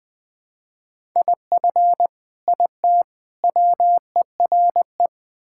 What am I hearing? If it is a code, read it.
Morse code, heard as a keyed tone, about 20 wpm, 700 Hz. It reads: IF IT WERE
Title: Build A Phrase - 5-Word Phrases - 20wpm